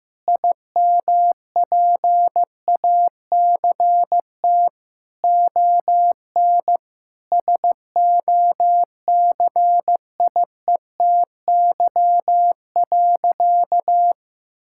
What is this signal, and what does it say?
Morse code, heard as a keyed tone, about 16 wpm, 700 Hz. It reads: IMPACT ON SOCIETY.
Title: Sentences from Top 1000 Words - Review Part 6 - 15wpm